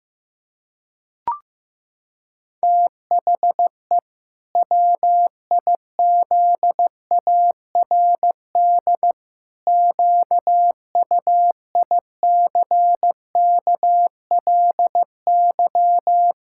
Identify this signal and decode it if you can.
Morse code, heard as a keyed tone, about 15 wpm, 700 Hz. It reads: THE WIZARD QUICKLY
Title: Pangram Challenge v2 - 15wpm